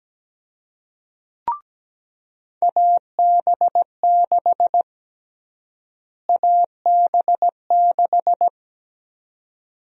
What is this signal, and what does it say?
Morse code, heard as a keyed tone, about 17 wpm, 700 Hz. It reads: AB6 AB6
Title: USA Call Sign Prefixes - Repeated 2x - 17wpm